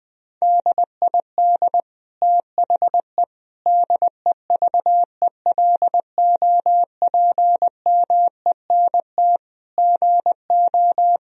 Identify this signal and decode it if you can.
Morse code, heard as a keyed tone, about 20 wpm, 700 Hz. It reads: DID THE DEVELOPMENT GO
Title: Sentences from Top 1000 Words - Review Part 4 - 20wpm